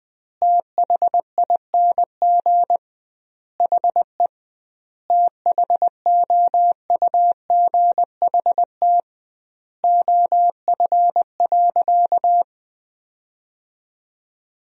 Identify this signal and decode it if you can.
Morse code, heard as a keyed tone, about 20 wpm, 700 Hz. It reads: THING HE THOUGHT OF.
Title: 2x Word-Spacing - Sentences from Top 300 Words - 20wpm